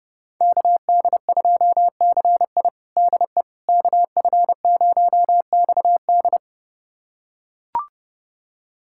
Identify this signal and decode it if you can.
Morse code, heard as a keyed tone, about 32 wpm, 700 Hz. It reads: KD2CI DE KF0XD
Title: USA Callsign DE Callsign - 5 Characters - 30wpm